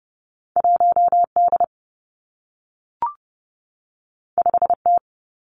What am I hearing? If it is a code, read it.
Morse code, heard as a keyed tone, about 30 wpm, 700 Hz. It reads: 1 D 5 T
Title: Two Challenging Letter-Number Combinations 30wpm